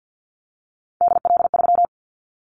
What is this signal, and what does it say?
Morse code, heard as a keyed tone, about 50 wpm, 700 Hz. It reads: DL3
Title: Callsign Prefixes for Top 25 DXCC - 50wpm